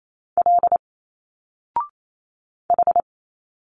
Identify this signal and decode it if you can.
Morse code, heard as a keyed tone, about 28 wpm, 700 Hz. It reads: L H
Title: Single Letters 28wpm